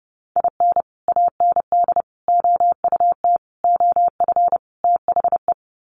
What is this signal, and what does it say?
Morse code, heard as a keyed tone, about 30 wpm, 700 Hz. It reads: IN AND OUT OF THE
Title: Sets of 5 Words 30wpm